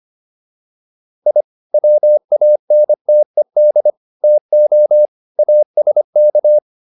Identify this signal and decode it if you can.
Morse code, heard as a keyed tone, about 25 wpm, 600 Hz. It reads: I WANTED TO ASK